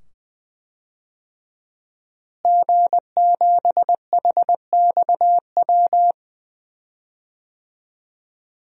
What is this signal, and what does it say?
Morse code, heard as a keyed tone, about 20 wpm, 700 Hz. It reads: G7HXW
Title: DXCC-100 Call Signs - 5 Characters - Repeated 2x - 20wpm